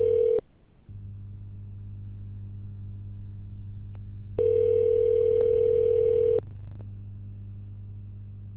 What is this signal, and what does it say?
Ringback tone (North America), 440 + 480 Hz.